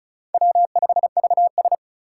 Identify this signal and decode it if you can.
Morse code, heard as a keyed tone, about 34 wpm, 700 Hz. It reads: W5VS
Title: USA Call Signs - 4 Characters - 4x Character Spacing - 35wpm